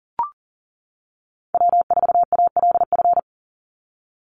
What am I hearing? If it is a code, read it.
Morse code, heard as a keyed tone, about 40 wpm, 700 Hz. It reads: W4ALF
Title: Call Signs - 40wpm